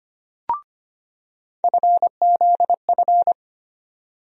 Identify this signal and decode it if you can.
Morse code, heard as a keyed tone, about 25 wpm, 700 Hz. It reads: FZF
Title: USA Call Sign Postfixes - 3 Letters 25wpm